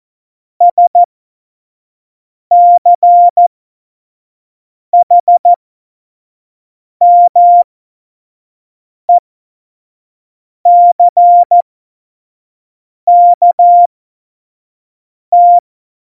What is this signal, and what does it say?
Morse code, heard as a keyed tone, about 14 wpm, spaced as for 5 wpm, 700 Hz. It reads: SCHMECKT
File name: Morsekurs / Witze / Witz 14. Januar 2020, 25 F70